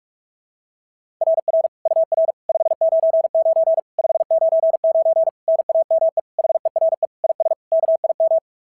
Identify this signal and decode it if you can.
Morse code, heard as a keyed tone, about 46 wpm, 690 Hz. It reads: RR UR 599 599 NAME HERE IS KIM
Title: SOTA QSO Practice 45wpm